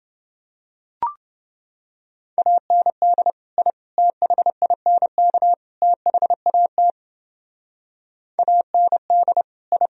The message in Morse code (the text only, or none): AND I THINK THAT AND I